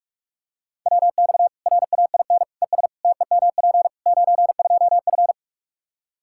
Morse code, heard as a keyed tone, about 43 wpm, 700 Hz. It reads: WX RAIN ES TEMP 92F
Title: Practice QSO - Exchange 2 - Info - 45wpm